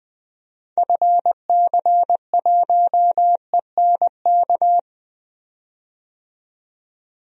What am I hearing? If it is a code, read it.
Morse code, heard as a keyed tone, about 20 wpm, 700 Hz. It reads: FC1ENK
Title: DXCC-100 Call Signs - 6 Characters - 20wpm